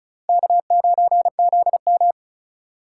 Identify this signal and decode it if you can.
Morse code, heard as a keyed tone, about 36 wpm, 700 Hz. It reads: K9ZM
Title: USA Call Signs - 4x Character Spacing - 35wpm